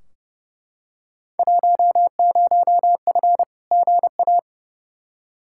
Morse code, heard as a keyed tone, about 31 wpm, 700 Hz. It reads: 10F GA
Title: Field Day Exchanges Encore - 30wpm